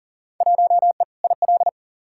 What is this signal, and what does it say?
Morse code, heard as a keyed tone, about 40 wpm, 700 Hz. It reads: 1E IL
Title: Field Day Exchanges - 40wpm